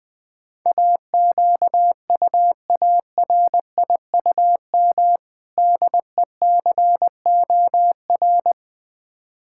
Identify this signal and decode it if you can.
Morse code, heard as a keyed tone, about 20 wpm, 690 Hz. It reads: AQUARIUM DECOR
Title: Difficult Words with Hints - 20wpm